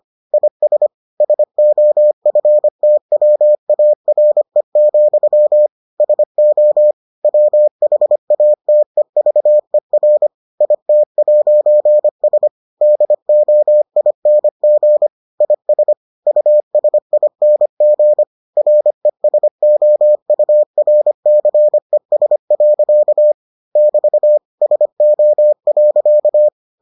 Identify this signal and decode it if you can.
Morse code, heard as a keyed tone, about 25 wpm, 600 Hz. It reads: IS SOFTWARE, SO WHATEVER IT'S DOING IS USING RESOURCES. <BT> SO.